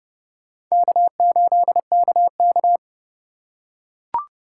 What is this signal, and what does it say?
Morse code, heard as a keyed tone, about 28 wpm, 700 Hz. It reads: K8KK